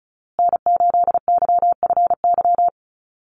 Morse code, heard as a keyed tone, about 34 wpm, 700 Hz. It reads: N8YFY
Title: USA Call Signs Repeated 2x 35wpm